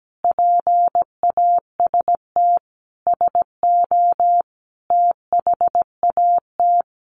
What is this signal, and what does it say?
Morse code, heard as a keyed tone, about 17 wpm, 700 Hz. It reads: PAST SO THAT
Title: A Sound Of Thunder 17wpm